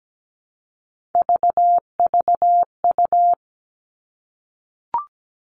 Morse code, heard as a keyed tone, about 17 wpm, 700 Hz. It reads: VVU
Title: All Cut Numbers - 3-Digit 17wpm